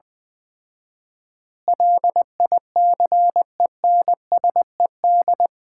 Morse code, heard as a keyed tone, about 20 wpm, 700 Hz. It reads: LICENSED
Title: Common QSO Phrases - 4x Word-Spacing - 20wpm